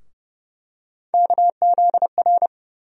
Morse code, heard as a keyed tone, about 30 wpm, 700 Hz. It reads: KZR
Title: USA Call Sign Postfixes - 3 Letters 30wpm